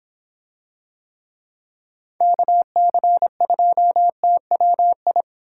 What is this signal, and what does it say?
Morse code, heard as a keyed tone, about 24 wpm, 700 Hz. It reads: KC2TWI